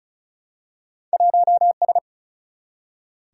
Morse code, heard as a keyed tone, about 36 wpm, 700 Hz. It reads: 1S
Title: Single-Letter Single-Number 35wpm